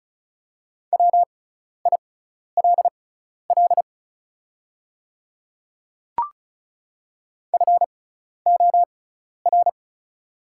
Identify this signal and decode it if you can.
Morse code, heard as a keyed tone, about 35 wpm, spaced as for 12 wpm, 690 Hz. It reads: WILL FOR
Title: 6x Character Spacing - Top 100 Words - 35wpm